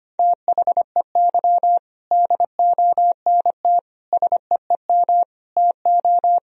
Morse code, heard as a keyed tone, about 25 wpm, 700 Hz. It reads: THEY DONT SEEM TO
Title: Sentences from Top 1000 Words - Review Part 1 - 25wpm